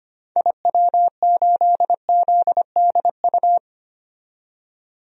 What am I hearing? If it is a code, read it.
Morse code, heard as a keyed tone, about 25 wpm, 700 Hz. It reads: IW8ZDU
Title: DXCC-100 Call Signs - 6 Characters - 25wpm